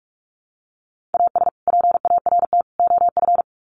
Morse code, heard as a keyed tone, about 44 wpm, 700 Hz. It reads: AS PART OF
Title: Sets of 3 Words 45wpm